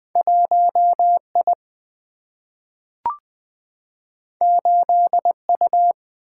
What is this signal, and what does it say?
Morse code, heard as a keyed tone, about 20 wpm, 700 Hz. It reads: 1I 8U
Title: Single-Letter Single-Number 20wpm